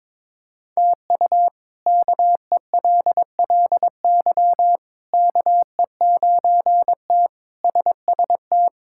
Morse code, heard as a keyed tone, about 22 wpm, 700 Hz. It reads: TU KELLY KE9T SST
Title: K1USN SST QSO Practice - 22wpm